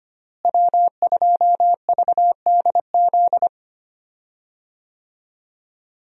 Morse code, heard as a keyed tone, about 25 wpm, 700 Hz. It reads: W2VDZ
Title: USA Call Signs - 5 Characters - Repeated 2x - 25wpm